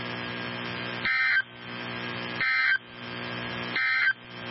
Emergency Alert System (SAME) tones, heard as harsh buzzing bursts. It is heard from 1.0 to 4.2 s.
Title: NOAA Weather Radio 162.550